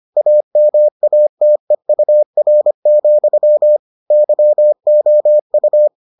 Morse code, heard as a keyed tone, about 25 wpm, 600 Hz. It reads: AMATEUR, YOU